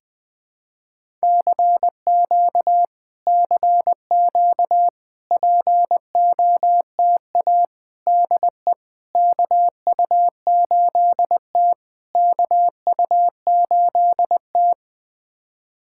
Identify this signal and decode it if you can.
Morse code, heard as a keyed tone, about 20 wpm, 700 Hz. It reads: CQ CQ POTA DE KU8T KU8T
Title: POTA QSO Practice - 20wpm